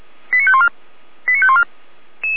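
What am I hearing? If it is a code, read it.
Five-tone selective call (ZVEI), heard as quick runs of five tones, tones of about 70 ms. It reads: 89514 89514
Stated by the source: ZVEI five-tone received by SDR, fire brigade Feuerwehr Orlamünde.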